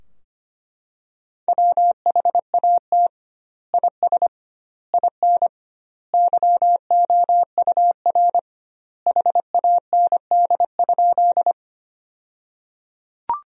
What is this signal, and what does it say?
Morse code, heard as a keyed tone, about 25 wpm, 700 Hz. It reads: WHAT IS IN YOUR HAND?
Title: 2x Word-Spacing - Sentences from Top 200 Words - 25wpm